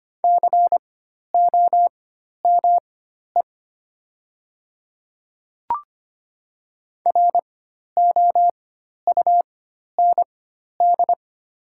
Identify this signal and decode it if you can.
Morse code, heard as a keyed tone, about 25 wpm, spaced as for 12 wpm, 700 Hz. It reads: COME ROUND
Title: 4x Character Spacing - Top 200 Words - 25wpm